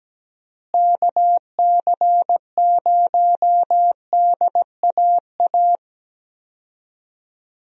Morse code, heard as a keyed tone, about 17 wpm, 700 Hz. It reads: KC0DAA
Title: USA Call Signs - 6 Characters - Repeated 3x - 17wpm